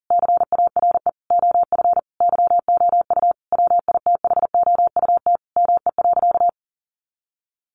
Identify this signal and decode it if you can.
Morse code, heard as a keyed tone, about 40 wpm, 700 Hz. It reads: CARE OF YOU WITHOUT ME.
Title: Sentences from Top 1200 Words 40wpm